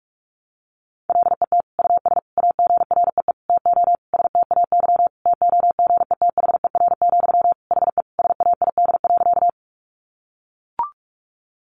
Morse code, heard as a keyed tone, about 44 wpm, 700 Hz. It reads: LET US AGREE TO STAY TOGETHER, HE SAID.
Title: Sentences from Top 1100 Words 45wpm